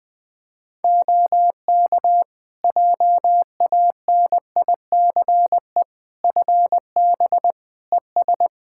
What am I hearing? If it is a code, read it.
Morse code, heard as a keyed tone, about 20 wpm, 700 Hz. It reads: OK JANICE FB ES
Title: K1USN SST QSO Practice 20wpm